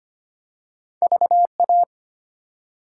Morse code, heard as a keyed tone, about 25 wpm, 700 Hz. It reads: VA
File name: practice / 25wpm / ARRL Sections v2 - 25wpm